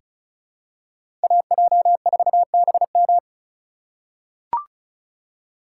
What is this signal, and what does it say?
Morse code, heard as a keyed tone, about 34 wpm, 690 Hz. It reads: AJ4BM
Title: USA Call Signs - 5 Characters - 2x Character Spacing - 35wpm